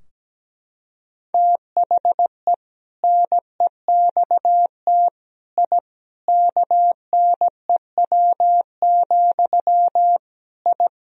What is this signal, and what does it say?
Morse code, heard as a keyed tone, about 17 wpm, 700 Hz. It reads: THE NEXT I KNEW, I
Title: Sentences from Top 1000 Words - Review Part 3 - 17wpm